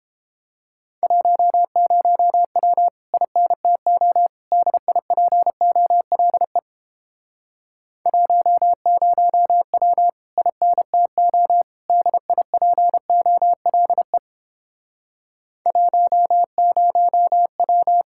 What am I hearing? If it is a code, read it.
Morse code, heard as a keyed tone, about 31 wpm, 700 Hz. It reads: 10W INTO DIPOLE 10W INTO DIPOLE 10W